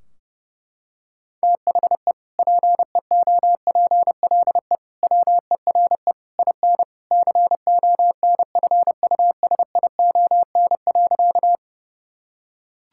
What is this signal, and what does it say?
Morse code, heard as a keyed tone, about 30 wpm, 700 Hz. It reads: THE PEOPLE WERE IN CONFUSION.
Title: Sentences from Top 1600 Words 30wpm